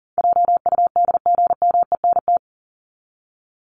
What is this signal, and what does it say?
Morse code, heard as a keyed tone, about 40 wpm, 700 Hz. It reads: JUDGMENT
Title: Top 1751-2000 Words 40wpm